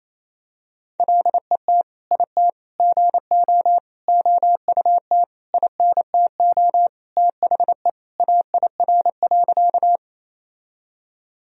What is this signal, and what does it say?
Morse code, heard as a keyed tone, about 28 wpm, 700 Hz. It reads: LET IT GO OUT INTO THE AIR.
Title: Sentences from Top 1000 Words - Review Part 1 - 28wpm